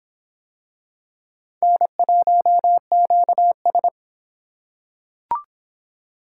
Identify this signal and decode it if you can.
Morse code, heard as a keyed tone, about 23 wpm, 700 Hz. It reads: N1QS